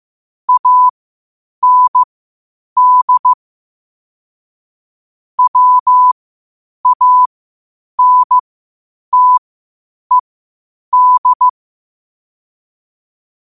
Morse code, heard as a keyed tone, about 15 wpm, spaced as for 8 wpm, 1000 Hz. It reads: AND WANTED
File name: qotdfiles / 8wpm